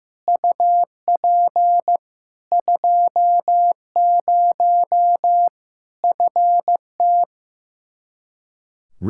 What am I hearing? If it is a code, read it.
Morse code, heard as a keyed tone, about 15 wpm, 700 Hz. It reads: UP 20 FT
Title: K1USN SST QSO Practice 15wpm